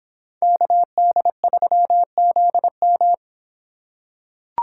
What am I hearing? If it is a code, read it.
Morse code, heard as a keyed tone, about 26 wpm, 700 Hz. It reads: KD3ZM